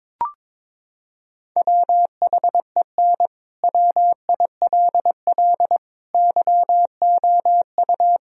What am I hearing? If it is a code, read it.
Morse code, heard as a keyed tone, about 22 wpm, 700 Hz. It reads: WHEN WILL YOU
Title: Sentences from Top 1000 Words - Review Part 4 - 22wpm